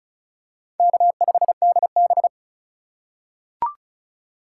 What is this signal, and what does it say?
Morse code, heard as a keyed tone, about 36 wpm, 700 Hz. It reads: K5DB
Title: USA Call Signs - 4 Characters - 4x Character Spacing - 35wpm